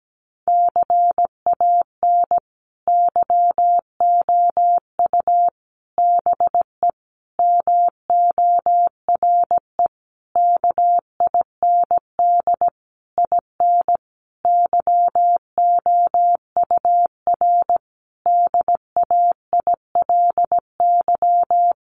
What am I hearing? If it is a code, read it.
Morse code, heard as a keyed tone, about 17 wpm, 700 Hz. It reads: CAN YOU BE MORE KIND IN YOUR DAILY
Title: Sentences from Top 1100 Words 17wpm